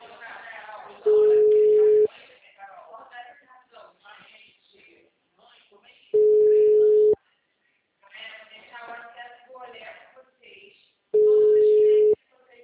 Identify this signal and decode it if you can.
Ringback tone (Europe, 425 Hz), with speech.
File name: out- -2034- - - .2864